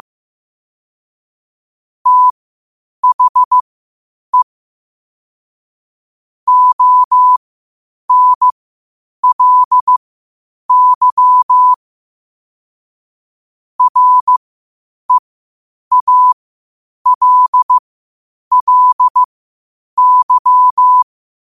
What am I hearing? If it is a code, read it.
Morse code, heard as a keyed tone, about 15 wpm, spaced as for 8 wpm, 1000 Hz. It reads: THE ONLY REALLY
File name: qotdfiles / 8wpm